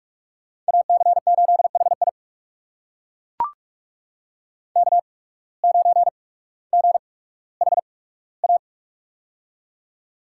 Morse code, heard as a keyed tone, about 41 wpm, spaced as for 19 wpm, 700 Hz. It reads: AC8HI K9GHA
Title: USA Call Signs - 5 Characters - 8x Character Spacing - 45wpm